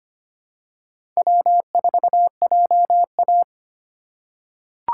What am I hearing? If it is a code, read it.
Morse code, heard as a keyed tone, about 25 wpm, 700 Hz. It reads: W4JA
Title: USA Call Signs - 4 Characters - 3x Character Spacing - 25wpm